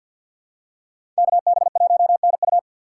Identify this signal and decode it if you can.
Morse code, heard as a keyed tone, about 53 wpm, 700 Hz. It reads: KB1NU
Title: USA Call Signs - 5 Characters - 6x Character Spacing - 50wpm